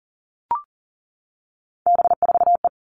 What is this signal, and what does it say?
Morse code, heard as a keyed tone, about 40 wpm, 700 Hz. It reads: B4E
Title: All Cut Numbers - 3-Digit 40wpm